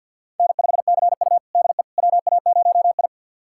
Morse code, heard as a keyed tone, about 52 wpm, 700 Hz. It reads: N5CU DE WA0I
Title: USA Callsign DE Callsign - 4 Characters - 50wpm